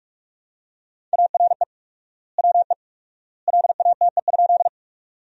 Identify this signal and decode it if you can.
Morse code, heard as a keyed tone, about 47 wpm, 700 Hz. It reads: ARE WE LATE?
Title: 4x Word-Spacing - Sentences from Top 300 Words - 45wpm